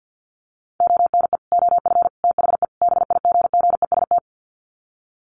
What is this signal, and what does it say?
Morse code, heard as a keyed tone, about 50 wpm, 700 Hz. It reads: ONE OF THE BIGGEST
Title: Sets of 4 Words 50wpm